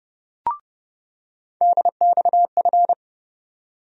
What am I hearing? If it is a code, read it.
Morse code, heard as a keyed tone, about 30 wpm, 700 Hz. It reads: DXF
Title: USA Call Sign Postfixes - 3 Letters 30wpm